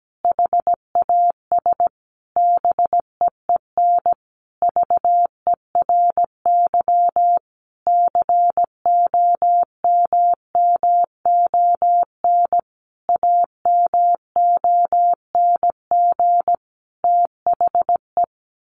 Morse code, heard as a keyed tone, about 17 wpm, 700 Hz. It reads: HAS BEEN VERY COMMON AMONG THE
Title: Sentences from Top 1400 Words 17wpm